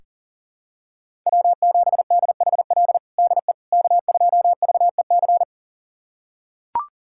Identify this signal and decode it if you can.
Morse code, heard as a keyed tone, about 40 wpm, 700 Hz. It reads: W7DHL DE K2VEC
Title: USA Callsign DE Callsign - 5 Characters - 40wpm